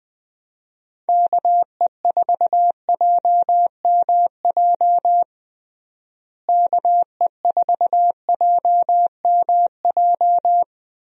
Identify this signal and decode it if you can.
Morse code, heard as a keyed tone, about 21 wpm, 700 Hz. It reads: KE4JMJ KE4JMJ
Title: USA Call Signs - 6 Characters - Repeated 2x - 20wpm